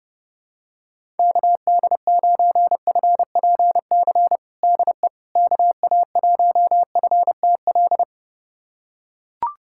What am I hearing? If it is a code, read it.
Morse code, heard as a keyed tone, about 30 wpm, 700 Hz. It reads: KD9FPC DE KA1FTL
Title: USA Callsign DE Callsign - 6 Characters - 30wpm